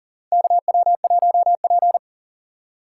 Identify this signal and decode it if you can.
Morse code, heard as a keyed tone, about 40 wpm, 700 Hz. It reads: KW1P
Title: USA Call Signs - 4 Characters - 2x Character Spacing - 40wpm